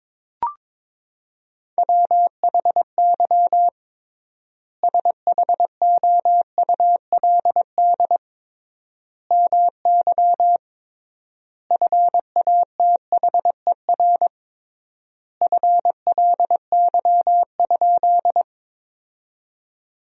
Morse code, heard as a keyed tone, about 22 wpm, 700 Hz. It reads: WHY SHOULD MY FATHER FLY?
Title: 3x Word-Spacing - Sentences from Top 500 Words - 22wpm